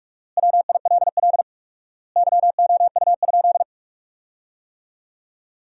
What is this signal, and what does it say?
Morse code, heard as a keyed tone, about 46 wpm, 700 Hz. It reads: WILL YOU?
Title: 4x Word-Spacing - Sentences from Top 300 Words - 45wpm